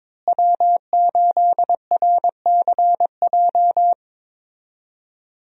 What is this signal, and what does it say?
Morse code, heard as a keyed tone, about 22 wpm, 700 Hz. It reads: W8RCJ
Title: USA Call Signs - 5 Characters - Repeated 3x - 22wpm